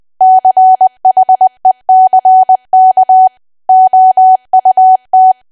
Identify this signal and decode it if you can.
Morse code, heard as a keyed tone, about 20 wpm, 750 Hz. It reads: CHECK OUT
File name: cwa / W1AW-Level3 / 20WPM